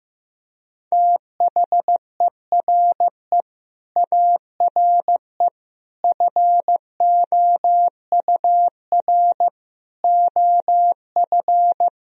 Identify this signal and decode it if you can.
Morse code, heard as a keyed tone, about 15 wpm, 700 Hz. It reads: THERE ARE FOUR OF
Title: Sentences from Top 1000 Words - Review Part 1 - 15wpm